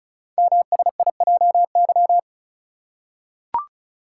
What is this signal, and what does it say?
Morse code, heard as a keyed tone, about 35 wpm, 700 Hz. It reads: MSIJY
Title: Code Groups - 5 Letters - 35wpm